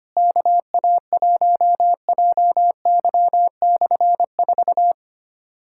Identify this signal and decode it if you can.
Morse code, heard as a keyed tone, about 25 wpm, 700 Hz. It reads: KA1JY/4
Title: K1USN SST QSO Practice - 25wpm